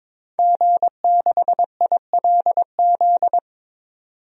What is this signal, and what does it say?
Morse code, heard as a keyed tone, about 22 wpm, 700 Hz. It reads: G6ILZ
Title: DXCC-100 Call Signs - 5 Characters - 22wpm